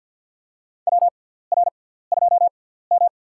Morse code, heard as a keyed tone, about 53 wpm, spaced as for 17 wpm, 700 Hz. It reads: WR2M
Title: USA Call Signs - 6x Character Spacing - 50wpm